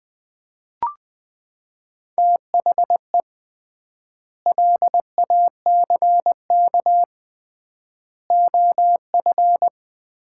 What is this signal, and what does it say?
Morse code, heard as a keyed tone, about 21 wpm, 700 Hz. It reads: THE LACK OF
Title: 3x Word-Spacing - Sets of 3 Words - 20wpm